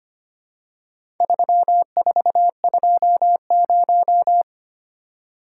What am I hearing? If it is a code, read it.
Morse code, heard as a keyed tone, about 26 wpm, 700 Hz. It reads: 3420
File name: practice / 25wpm / Four Letter-Number Combinations - 25wpm